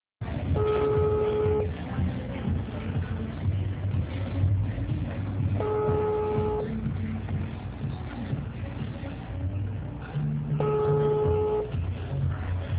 Ringback tone (Europe, 425 Hz), with noise.
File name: out- -2008- - - .3744